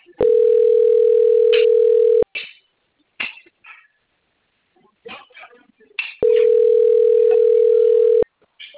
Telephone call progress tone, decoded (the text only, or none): ringback tone